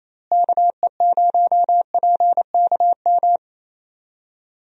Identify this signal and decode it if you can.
Morse code, heard as a keyed tone, about 28 wpm, 700 Hz. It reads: KE0PKM
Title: USA Call Signs - 6 Characters - Repeated 3x - 28wpm